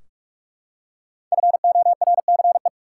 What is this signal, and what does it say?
Morse code, heard as a keyed tone, about 46 wpm, 700 Hz. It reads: FORCE
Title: Five Letter Words - 45wpm